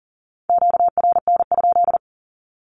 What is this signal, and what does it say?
Morse code, heard as a keyed tone, about 40 wpm, 700 Hz. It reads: QRN?
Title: Q-Codes Repeated 3x 40wpm